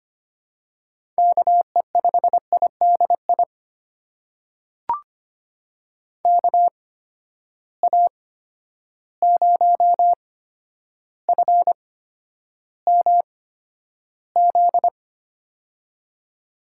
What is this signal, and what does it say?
Morse code, heard as a keyed tone, about 25 wpm, spaced as for 11 wpm, 700 Hz. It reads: KE5IDI KA0FMZ
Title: USA Call Signs - 6 Characters - 8x Character Spacing - 25wpm